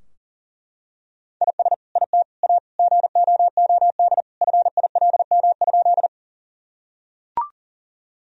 Morse code, heard as a keyed tone, about 40 wpm, 700 Hz. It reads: IS IT A GOOD FILM?
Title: Sentences from Top 1000 Words - Review Part 3 - 40wpm